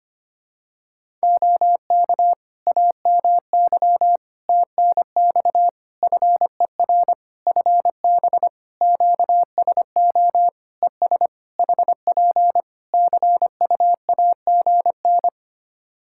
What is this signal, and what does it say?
Morse code, heard as a keyed tone, about 25 wpm, 700 Hz. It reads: OK AMY TNX FER FB QSO ES HP CUAGN
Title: Practice QSO - Exchange 4 - Ending - 25wpm